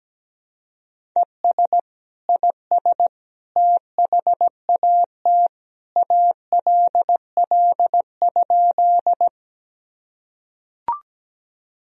Morse code, heard as a keyed tone, about 17 wpm, 700 Hz. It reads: ES IS THAT ALL?
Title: Sentences from Top 100 Words and Abbreviations 17wpm